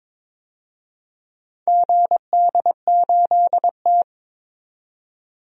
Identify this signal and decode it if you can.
Morse code, heard as a keyed tone, about 22 wpm, 700 Hz. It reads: GD8T
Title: DXCC-100 Call Signs - 4 Characters - 22wpm